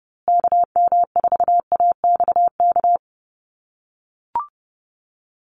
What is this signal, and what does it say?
Morse code, heard as a keyed tone, about 30 wpm, 700 Hz. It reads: KM4AXK
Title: USA Call Signs Repeated 3x 30wpm